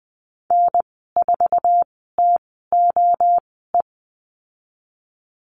Morse code, heard as a keyed tone, about 20 wpm, spaced as for 15 wpm, 700 Hz. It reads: N4TOE
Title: 055 Lesson 10g - Letter r - Call Signs 20wpm